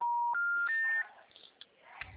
Special information tone.
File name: out- -2025- - - .52